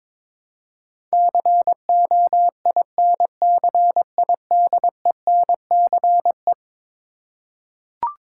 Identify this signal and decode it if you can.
Morse code, heard as a keyed tone, about 21 wpm, 700 Hz. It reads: COINCIDENCE
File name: practice / 22wpm / Eleven Letter Words - 22wpm